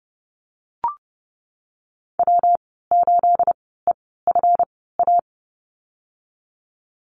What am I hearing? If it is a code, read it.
Morse code, heard as a keyed tone, about 30 wpm, spaced as for 17 wpm, 700 Hz. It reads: W8EFA
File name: Lesson 37g - Letter x - Call Signs 30wpm